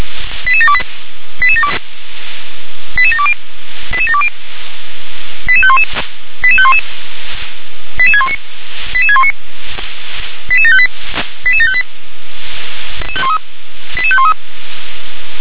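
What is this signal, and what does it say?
Five-tone selective call (ZVEI), heard as quick runs of five tones, tones of about 70 ms. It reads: 88728 88720 88720 88511 88511 88619 88619 88758 88758 88513
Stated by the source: Received on an SDR.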